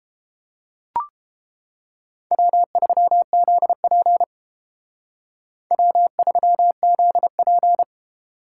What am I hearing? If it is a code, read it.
Morse code, heard as a keyed tone, about 31 wpm, 700 Hz. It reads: W3ZP W3ZP